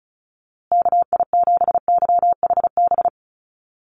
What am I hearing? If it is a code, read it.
Morse code, heard as a keyed tone, about 34 wpm, 700 Hz. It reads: KI7YHB
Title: USA Call Signs Repeated 3x 35wpm